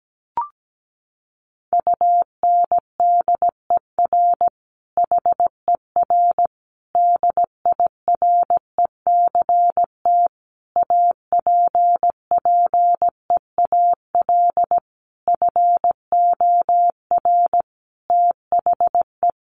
Morse code, heard as a keyed tone, about 17 wpm, 700 Hz. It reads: UNDER HER DIRECT APPEAL FOR THE
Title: Sentences from Top 1300 Words 17wpm